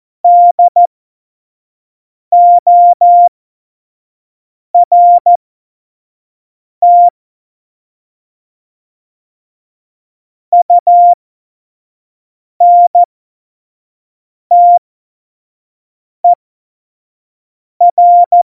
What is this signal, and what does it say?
Morse code, heard as a keyed tone, about 14 wpm, spaced as for 5 wpm, 700 Hz. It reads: DORT UNTER
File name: Morsekurs / Witze / Witz 30. Oktober 2018, 25 F70